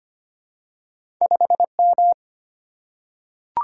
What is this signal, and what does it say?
Morse code, heard as a keyed tone, about 25 wpm, 700 Hz. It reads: 5M
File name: Two Letter-Number Combinations - 25wpm